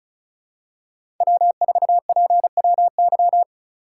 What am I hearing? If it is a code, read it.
Morse code, heard as a keyed tone, about 34 wpm, 700 Hz. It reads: W4PWY
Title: USA Call Signs - 5 Characters - 4x Character Spacing - 35wpm